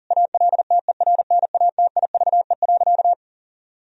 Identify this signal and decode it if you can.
Morse code, heard as a keyed tone, about 40 wpm, 700 Hz. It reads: ALTERNATIVE.
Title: Sentences from Top 1000 Words - Review Part 4 - 40wpm